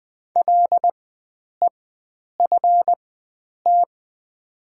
Morse code, heard as a keyed tone, about 20 wpm, spaced as for 9 wpm, 700 Hz. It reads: LEFT
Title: 4x Character Spacing - Top 500 Words - 20wpm